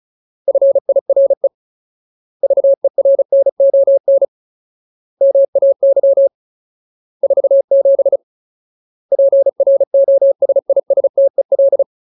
Morse code, heard as a keyed tone, about 34 wpm, 550 Hz. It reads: FIRE VERNON MAY 47 PROSISTEL
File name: Ultimate Word List / 35WPM Ultimate